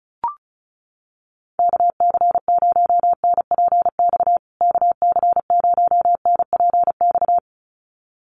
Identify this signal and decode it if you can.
Morse code, heard as a keyed tone, about 34 wpm, 700 Hz. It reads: KC0NPX KC0NPX
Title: USA Call Signs Repeated 2x 35wpm